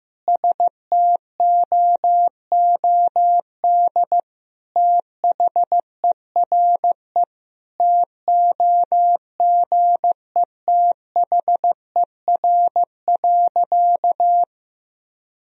Morse code, heard as a keyed tone, about 15 wpm, 700 Hz. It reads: STOOD THERE TOGETHER.
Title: Sentences from Top 1000 Words - Review Part 2 - 15wpm